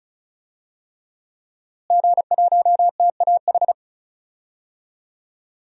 Morse code, heard as a keyed tone, about 34 wpm, 700 Hz. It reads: G1TAH
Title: DXCC-100 Call Signs - 5 Characters - Repeated 3x - 35wpm